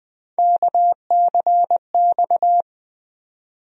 Morse code, heard as a keyed tone, about 21 wpm, 700 Hz. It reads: KCX
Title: USA Call Sign Postfixes - 3 Letters 20wpm